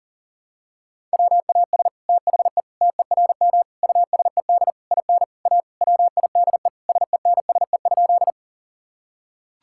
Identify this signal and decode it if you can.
Morse code, heard as a keyed tone, about 37 wpm, 700 Hz. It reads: WAS THE TERM USED IN A WIDE SENSE?
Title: Sentences from Top 1000 Words - Review Part 6 - 40wpm